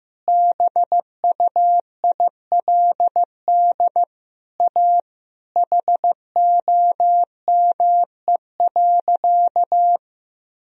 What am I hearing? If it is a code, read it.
Morse code, heard as a keyed tone, about 15 wpm, 700 Hz. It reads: BUILD A HOME.
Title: Sentences from Top 1000 Words - Review Part 1 - 15wpm